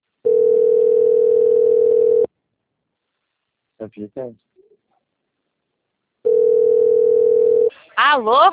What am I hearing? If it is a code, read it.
Ringback tone (North America), 440 + 480 Hz, with speech.
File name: out- - - - - .33671